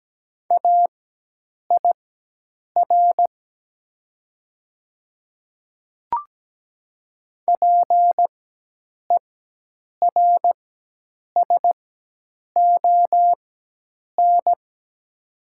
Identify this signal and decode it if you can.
Morse code, heard as a keyed tone, about 17 wpm, spaced as for 8 wpm, 700 Hz. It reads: AIR PERSON
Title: 4x Character Spacing - Top 500 Words - 17wpm